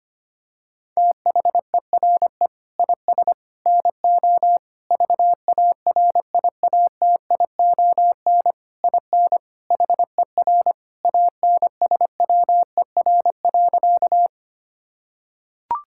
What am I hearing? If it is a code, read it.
Morse code, heard as a keyed tone, about 25 wpm, 700 Hz. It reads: THERE IS NO VARIATION IN HER ANSWER.
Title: Sentences from Top 1600 Words 25wpm